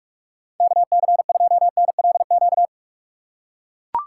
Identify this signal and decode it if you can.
Morse code, heard as a keyed tone, about 44 wpm, 700 Hz. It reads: KC2NLQ